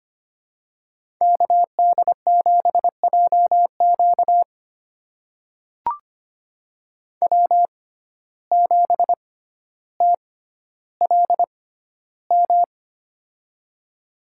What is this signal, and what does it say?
Morse code, heard as a keyed tone, about 25 wpm, spaced as for 13 wpm, 700 Hz. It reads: KD7JQ W7TLM
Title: USA Call Signs - 5 Characters - 6x Character Spacing - 25wpm